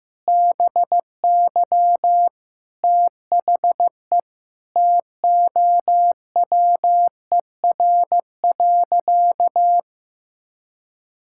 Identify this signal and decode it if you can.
Morse code, heard as a keyed tone, about 15 wpm, 700 Hz. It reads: BY THE TOWER.
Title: Pangram Challenge v2 - 15wpm